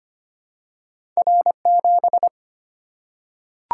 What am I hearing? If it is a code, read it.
Morse code, heard as a keyed tone, about 25 wpm, 700 Hz. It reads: R7
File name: Two Letter-Number Combinations - 25wpm